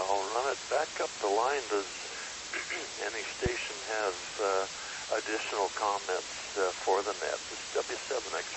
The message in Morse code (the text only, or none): none